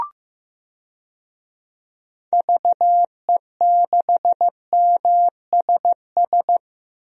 Morse code, heard as a keyed tone, about 15 wpm, 700 Hz. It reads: VE6MSS
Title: DXCC-100 Call Signs - 6 Characters - 15wpm